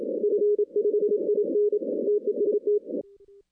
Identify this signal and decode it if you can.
Morse code, heard as a keyed tone, about 27 wpm, 410 Hz, with radio noise. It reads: F5IN TEST
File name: 2016 arrl dx cw / 2016-ARRL-DX-CW-K5TR- - 0-30